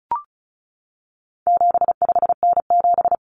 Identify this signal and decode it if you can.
Morse code, heard as a keyed tone, about 34 wpm, 700 Hz. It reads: 75N7
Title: Common Cut Numbers - 4-Digit 35wpm